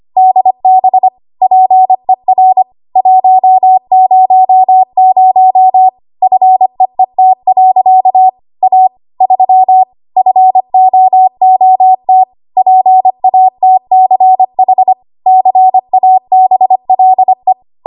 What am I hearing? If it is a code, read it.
Morse code, heard as a keyed tone, about 25 wpm, 750 Hz. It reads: DB PER 100 FEET. A 3 FOOT PATCH CABLE
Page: W1AW Web Code Practice Files